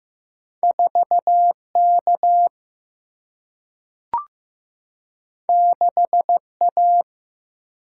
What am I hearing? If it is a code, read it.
Morse code, heard as a keyed tone, about 15 wpm, 700 Hz. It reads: 4 K 6 A
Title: Single-Letter Single-Number 15wpm